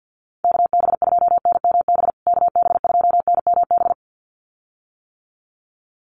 Morse code, heard as a keyed tone, about 50 wpm, 700 Hz. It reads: KB2NMB KB2NMB
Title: USA Call Signs Repeated 2x 50wpm